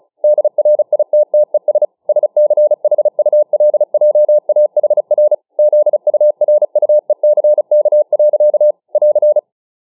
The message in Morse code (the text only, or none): DRITTES SCHULJAHR ZURUECK. <AR>